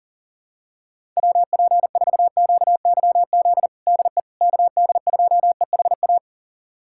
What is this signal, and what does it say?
Morse code, heard as a keyed tone, about 40 wpm, 700 Hz. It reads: WP4QYZ DE KD2EHA
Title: USA Callsign DE Callsign - 6 Characters - 40wpm